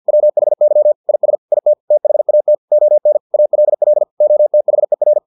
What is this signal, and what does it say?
Morse code, heard as a keyed tone, about 50 wpm, 600 Hz. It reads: WHY IS IT THAT ON ALL OTHER